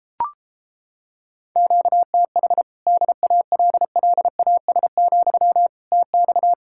Morse code, heard as a keyed tone, about 29 wpm, 700 Hz. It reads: QTH DALLAS, TX